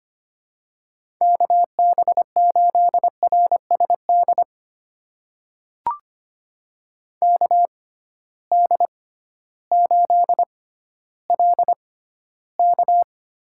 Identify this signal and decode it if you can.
Morse code, heard as a keyed tone, about 25 wpm, spaced as for 13 wpm, 700 Hz. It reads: KB8RSD KD8LK
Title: USA Call Signs - 6 Characters - 6x Character Spacing - 25wpm